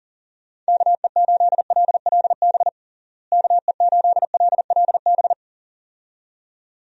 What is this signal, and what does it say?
Morse code, heard as a keyed tone, about 40 wpm, 700 Hz. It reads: KE8LLB KE8LLB
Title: USA Call Signs - 6 Characters - Repeated 2x - 40wpm